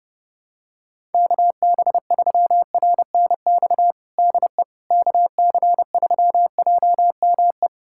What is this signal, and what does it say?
Morse code, heard as a keyed tone, about 32 wpm, 700 Hz. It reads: KB3RNX DE KC3JME
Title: USA Callsign DE Callsign 30wpm